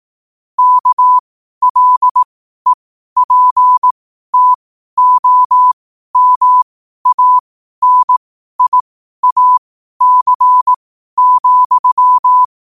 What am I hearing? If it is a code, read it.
Morse code, heard as a keyed tone, about 18 wpm, spaced as for 12 wpm, 1000 Hz. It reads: KLEPTOMANIAC,